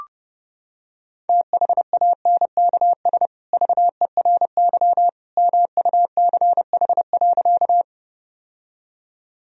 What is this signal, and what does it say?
Morse code, heard as a keyed tone, about 32 wpm, 700 Hz. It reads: THANKS VERY MUCH.
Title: Sentences from Top 1000 Words - Review Part 3 - 30wpm